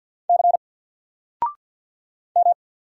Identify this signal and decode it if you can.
Morse code, heard as a keyed tone, about 48 wpm, 700 Hz. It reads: C M
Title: Single Letter-Number 50wpm